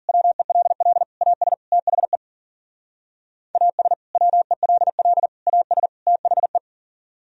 Morse code, heard as a keyed tone, about 52 wpm, 700 Hz. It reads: WELL AS THE AS WELL AS THE